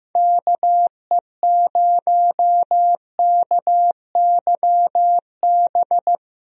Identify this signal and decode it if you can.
Morse code, heard as a keyed tone, about 15 wpm, 700 Hz. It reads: KE0KYB
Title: USA Call Signs - 6 Characters - Repeated 3x - 15wpm